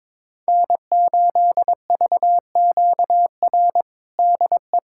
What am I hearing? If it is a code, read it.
Morse code, heard as a keyed tone, about 22 wpm, 700 Hz. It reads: N8VQR DE
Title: USA Callsign DE Callsign - 5 Characters - 22wpm